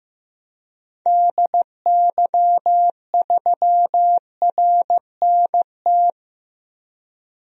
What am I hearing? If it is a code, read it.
Morse code, heard as a keyed tone, about 15 wpm, 700 Hz. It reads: DY3RNT
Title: DXCC-100 Call Signs - 6 Characters - 15wpm